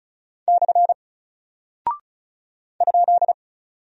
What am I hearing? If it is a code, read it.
Morse code, heard as a keyed tone, about 36 wpm, 700 Hz. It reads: / ?
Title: Single Character (Letter:Number:Punctuation:Prosign) Alt 35wpm